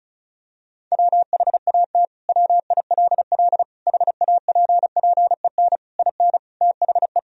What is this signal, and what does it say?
Morse code, heard as a keyed tone, about 36 wpm, 700 Hz. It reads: WHAT WILL HAPPEN IN THE
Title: Sentences from Top 1000 Words - Review Part 2 - 35wpm